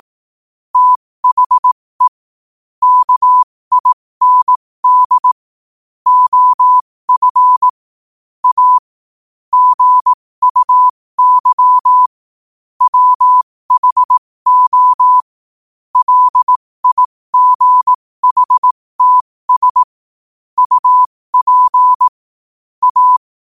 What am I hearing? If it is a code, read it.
Morse code, heard as a keyed tone, about 18 wpm, spaced as for 15 wpm, 1000 Hz. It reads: THE KIND OF A GUY WHO LIGHTS UP A
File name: qotdfiles / 15wpm